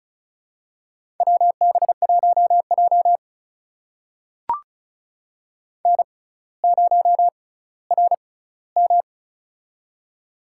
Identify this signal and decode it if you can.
Morse code, heard as a keyed tone, about 34 wpm, spaced as for 18 wpm, 700 Hz. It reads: WB1J N0RM
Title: USA Call Signs - 4 Characters - 6x Character Spacing - 35wpm